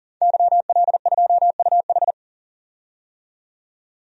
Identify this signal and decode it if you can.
Morse code, heard as a keyed tone, about 37 wpm, 700 Hz. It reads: YL2UH
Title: DXCC-100 Call Signs - 5 Characters - 40wpm